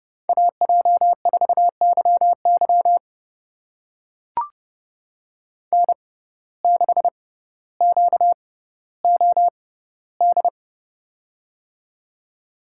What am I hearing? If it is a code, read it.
Morse code, heard as a keyed tone, about 30 wpm, spaced as for 15 wpm, 700 Hz. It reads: AJ4YY N6QOD
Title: USA Call Signs - 5 Characters - 6x Character Spacing - 30wpm